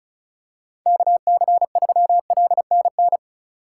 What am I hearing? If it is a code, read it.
Morse code, heard as a keyed tone, about 36 wpm, 700 Hz. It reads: KC3LNN
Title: USA Call Signs - 6 Characters - 4x Character Spacing - 35wpm